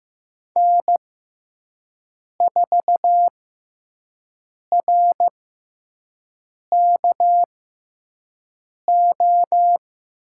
Morse code, heard as a keyed tone, about 15 wpm, spaced as for 5 wpm, 700 Hz. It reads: N4RKO
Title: USA Call Signs - 5 Characters - 6x Character Spacing - 15wpm